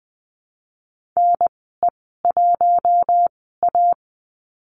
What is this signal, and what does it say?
Morse code, heard as a keyed tone, about 20 wpm, spaced as for 14 wpm, 700 Hz. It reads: NE1A
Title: 042 Lesson 8g - Number 1 - Call Signs 20wpm